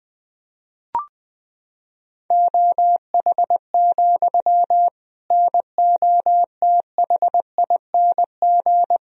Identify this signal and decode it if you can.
Morse code, heard as a keyed tone, about 20 wpm, 700 Hz. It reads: OH, NOTHING
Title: Sentences from Top 1000 Words - Review Part 3 - 20wpm